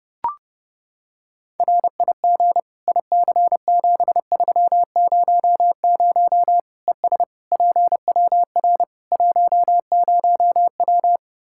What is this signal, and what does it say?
Morse code, heard as a keyed tone, about 30 wpm, 700 Hz. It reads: RIG IC7300 ES PWR 10W
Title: K1USN SST QSO Practice 30wpm